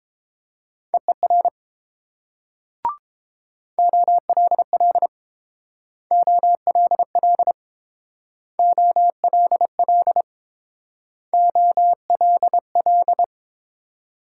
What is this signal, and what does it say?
Morse code, heard as a keyed tone, about 30 wpm, 700 Hz. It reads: EER OLL OLL OLL OLL